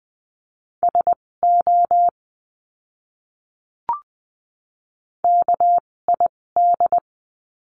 Morse code, heard as a keyed tone, about 20 wpm, spaced as for 16 wpm, 700 Hz. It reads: SO KID
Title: 256 Lesson 42a - Words 12-20wpm